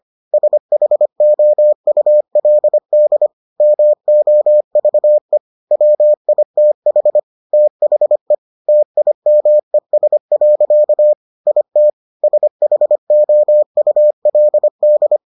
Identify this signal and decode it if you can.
Morse code, heard as a keyed tone, about 25 wpm, 600 Hz. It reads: SHOULD MOVE WITH THE TIMES. IT SHOULD